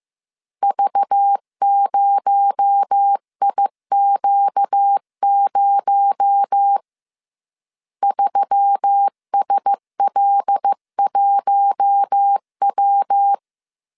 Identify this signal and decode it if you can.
Morse code, heard as a keyed tone, about 15 wpm, 800 Hz. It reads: V0IQ0 3SL1W
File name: CW / DET / 1 6 1500C S C 14wpm 2951